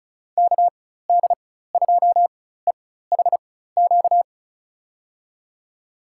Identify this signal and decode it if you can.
Morse code, heard as a keyed tone, about 36 wpm, spaced as for 16 wpm, 700 Hz. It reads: KD2EHQ
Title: USA Call Signs - 6 Characters - 4x Character Spacing - 35wpm